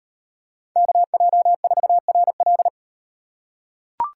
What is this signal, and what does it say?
Morse code, heard as a keyed tone, about 38 wpm, 700 Hz. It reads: KJ4RL